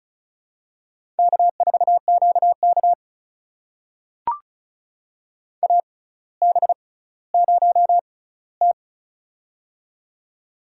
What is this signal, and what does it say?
Morse code, heard as a keyed tone, about 36 wpm, spaced as for 18 wpm, 700 Hz. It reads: K4QK AB0T
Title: USA Call Signs - 4 Characters - 6x Character Spacing - 35wpm